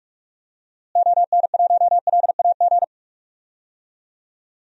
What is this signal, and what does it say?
Morse code, heard as a keyed tone, about 46 wpm, 700 Hz. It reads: ON1LAG
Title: DXCC-100 Call Signs - 6 Characters - Repeated 3x - 45wpm